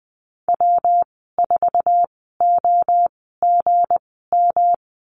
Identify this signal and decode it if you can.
Morse code, heard as a keyed tone, about 20 wpm, spaced as for 15 wpm, 700 Hz. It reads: W4OGM
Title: 173 Lesson 28g - Number 9 - Call Signs 20wpm